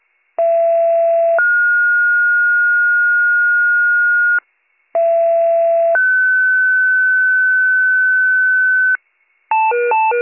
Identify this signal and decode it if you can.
Two-tone page, heard as a one-second tone, then a three-second tone, about 670 then 1430 Hz, about 670 then 1500 Hz. It is heard from 0.4 to 9.0 s.